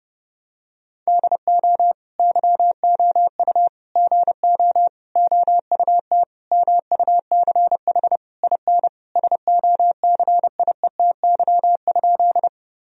Morse code, heard as a keyed tone, about 30 wpm, 700 Hz. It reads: DO YOU GO OUT MUCH IN SOCIETY?
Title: Sentences from Top 1000 Words - Review Part 4 - 30wpm